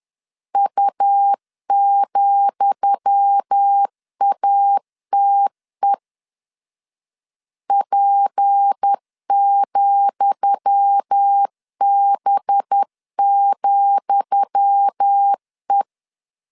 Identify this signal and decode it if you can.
Morse code, heard as a keyed tone, about 10 wpm, 800 Hz. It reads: U,ATE P,B,E
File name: LEZIONI / 10 SIGNS / 10D COMA 10wpm 3013